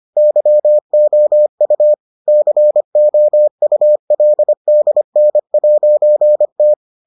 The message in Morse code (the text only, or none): YOU COULDN'T